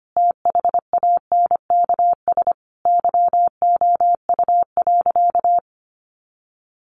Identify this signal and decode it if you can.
Morse code, heard as a keyed tone, about 25 wpm, 700 Hz. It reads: THANKS YOU.
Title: Sentences from Top 600 Words 25wpm